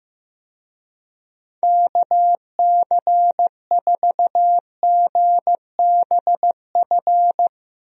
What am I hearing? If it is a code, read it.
Morse code, heard as a keyed tone, about 15 wpm, 700 Hz. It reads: KC4GBF
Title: USA Call Signs - 6 Characters - Repeated 3x - 15wpm